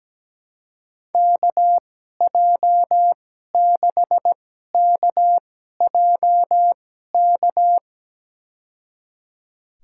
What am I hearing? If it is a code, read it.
Morse code, heard as a keyed tone, about 17 wpm, spaced as for 12 wpm, 700 Hz. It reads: KJ6KJK
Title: USA Call Signs - 6 Characters - 2x Character Spacing - 17wpm